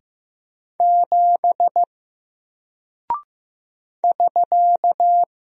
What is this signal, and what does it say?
Morse code, heard as a keyed tone, about 15 wpm, 700 Hz. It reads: 7 <SK>
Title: Single Character (Letter:Number:Punctuation:Prosign) Alt 15wpm